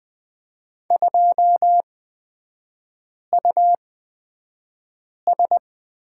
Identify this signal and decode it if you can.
Morse code, heard as a keyed tone, about 20 wpm, spaced as for 5 wpm, 700 Hz. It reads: 2US
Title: 243 Lesson 39e - Letter z - Three Characters 20wpm